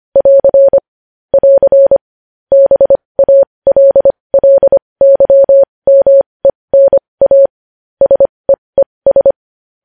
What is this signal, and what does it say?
Morse code, heard as a keyed tone, about 25 wpm, spaced as for 20 wpm, 550 Hz. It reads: <AR> <AR> BALLYMENA SEES
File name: podcast / News headlines-1x-25x20